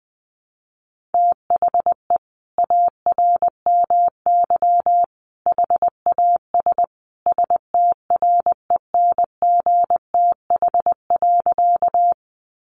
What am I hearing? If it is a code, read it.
Morse code, heard as a keyed tone, about 20 wpm, 700 Hz. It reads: THE ARMY HAS STRENGTH.
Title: Sentences from Top 1300 Words 20wpm